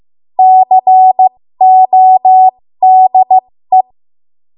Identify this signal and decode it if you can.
Morse code, heard as a keyed tone, about 15 wpm, spaced as for 13 wpm, 750 Hz. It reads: CODE
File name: Morse / 13WPM